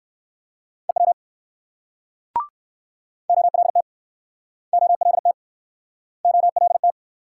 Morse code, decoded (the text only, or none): EW OLT OLT OLT